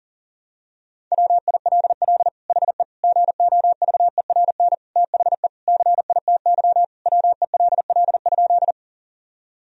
Morse code, heard as a keyed tone, about 40 wpm, 700 Hz. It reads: WILL HE GOVERN THE CITY WELL?
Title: Sentences from Top 1000 Words - Review Part 2 - 40wpm